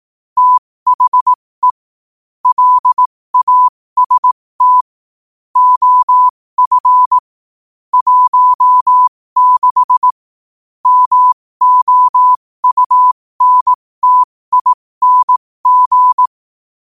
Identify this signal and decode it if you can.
Morse code, heard as a keyed tone, about 18 wpm, spaced as for 15 wpm, 1000 Hz. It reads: THE LAST OF 16 MOUNTING